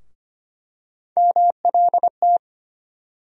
Morse code, heard as a keyed tone, about 25 wpm, 700 Hz. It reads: MLT
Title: USA Call Sign Postfixes - 3 Letters 25wpm